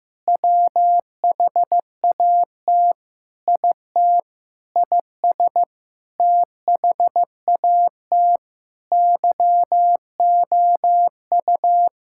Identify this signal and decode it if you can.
Morse code, heard as a keyed tone, about 15 wpm, 700 Hz. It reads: WHAT IT IS THAT YOU
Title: Sentences from Top 1000 Words - Review Part 6 - 15wpm